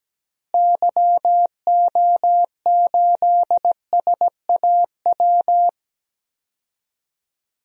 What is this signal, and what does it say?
Morse code, heard as a keyed tone, about 17 wpm, 700 Hz. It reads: YO8SAW
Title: DXCC-100 Call Signs - 6 Characters - Repeated 3x - 17wpm